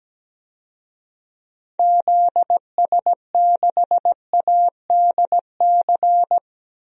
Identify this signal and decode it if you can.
Morse code, heard as a keyed tone, about 17 wpm, 700 Hz. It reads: ZS6ADC
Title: DXCC-100 Call Signs - 6 Characters - 17wpm